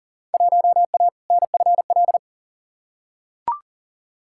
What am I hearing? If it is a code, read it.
Morse code, heard as a keyed tone, about 40 wpm, 700 Hz. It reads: 1A NFL
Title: Field Day Exchanges - 40wpm